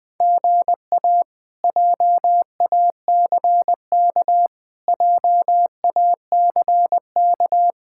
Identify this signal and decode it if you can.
Morse code, heard as a keyed tone, about 20 wpm, 700 Hz. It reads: GA JACK JACK
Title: K1USN SST QSO Practice - 20wpm